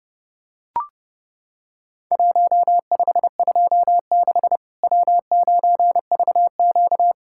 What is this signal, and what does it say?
Morse code, heard as a keyed tone, about 30 wpm, 700 Hz. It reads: 1526 W9VQ
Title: Worked All Europe DX Contest - QTC Exchange - 30wpm